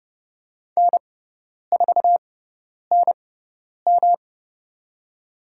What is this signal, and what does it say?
Morse code, heard as a keyed tone, about 30 wpm, spaced as for 10 wpm, 700 Hz. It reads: N4NM
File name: Lesson 18g - Letter m - Call Signs 30wpm